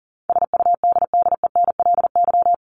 Morse code, heard as a keyed tone, about 40 wpm, 700 Hz. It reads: SUDDENLY
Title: Top 1501-1750 Words 40wpm